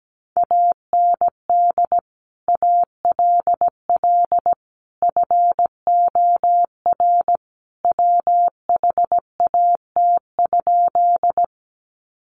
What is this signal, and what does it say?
Morse code, heard as a keyed tone, about 17 wpm, 700 Hz. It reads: AND ALL FOR WHAT?
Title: Sentences from Top 100 Words 17wpm